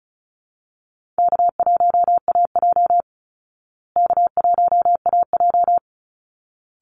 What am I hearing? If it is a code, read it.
Morse code, heard as a keyed tone, about 34 wpm, 700 Hz. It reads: K1AJ K1AJ
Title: Easy Call Signs Repeated 2x - 35wpm